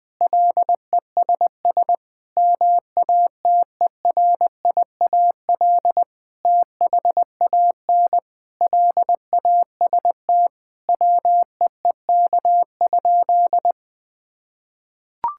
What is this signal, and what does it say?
Morse code, heard as a keyed tone, about 20 wpm, 700 Hz. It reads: LESS MATERIAL THAN LAST WEEK?
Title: Sentences from Top 1000 Words - Review Part 5 - 20wpm